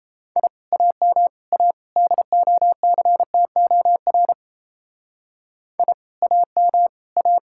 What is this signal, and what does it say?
Morse code, heard as a keyed tone, about 33 wpm, 700 Hz. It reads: I AM A DOCTOR I AM A
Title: Speed-Racing Common QSO Phrases - 28wpm